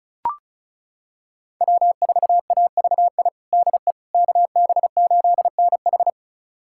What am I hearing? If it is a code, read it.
Morse code, heard as a keyed tone, about 36 wpm, 700 Hz. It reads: W4AVI DE KB8NH
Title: USA Callsign DE Callsign - 5 Characters - 35wpm